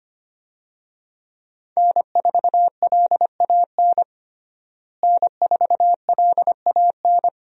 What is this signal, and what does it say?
Morse code, heard as a keyed tone, about 25 wpm, 690 Hz. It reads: N4LAN N4LAN
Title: USA Call Signs - 5 Characters - Repeated 2x - 25wpm